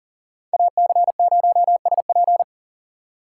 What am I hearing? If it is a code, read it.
Morse code, heard as a keyed tone, about 40 wpm, 700 Hz. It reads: AC0SP
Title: USA Call Signs - 4x Character Spacing - 40wpm